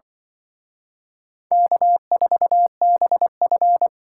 Morse code, heard as a keyed tone, about 25 wpm, 700 Hz. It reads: K4BF
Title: Speed Racing - USA 1x2 Call Signs 20wpm